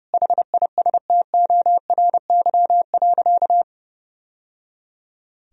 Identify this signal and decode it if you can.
Morse code, heard as a keyed tone, about 30 wpm, 700 Hz. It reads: HISTORY.
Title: Sentences from Top 1000 Words - Review Part 2 - 30wpm